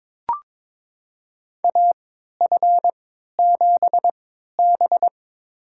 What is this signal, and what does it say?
Morse code, heard as a keyed tone, about 22 wpm, spaced as for 12 wpm, 700 Hz. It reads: AF7B
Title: USA Call Signs - 4 Characters - 3x Character Spacing - 22wpm